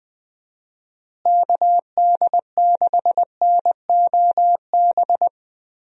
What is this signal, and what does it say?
Morse code, heard as a keyed tone, about 20 wpm, 700 Hz. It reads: KD6NOB
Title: USA Call Signs - 6 Characters - Repeated 3x - 20wpm